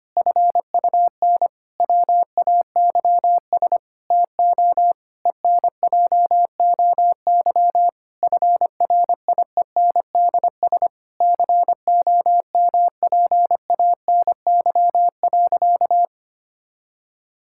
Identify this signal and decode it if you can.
Morse code, heard as a keyed tone, about 25 wpm, 700 Hz. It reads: FUN WAYS TO ENJOY FRIENDS COMPANY.
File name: practice / 25wpm / Difficult Sentences with Hints - 25wpm